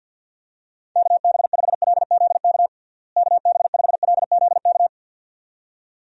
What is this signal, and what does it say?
Morse code, heard as a keyed tone, about 51 wpm, 700 Hz. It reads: KB5LZK KB5LZK
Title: USA Call Signs - 6 Characters - Repeated 2x - 50wpm